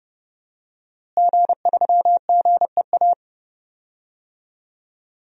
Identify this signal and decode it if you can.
Morse code, heard as a keyed tone, about 32 wpm, 700 Hz. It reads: G3GEA
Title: DXCC-100 Call Signs - 5 Characters - Repeated 3x - 30wpm